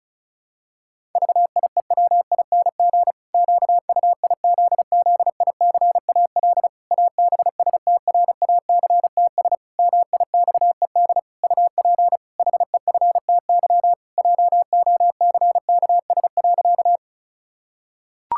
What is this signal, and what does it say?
Morse code, heard as a keyed tone, about 36 wpm, 700 Hz. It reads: VIEWING QUIZZICAL ABSTRACTS MIXED UP HEFTY JOCKS.
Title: Pangram Challenge v2 - 35wpm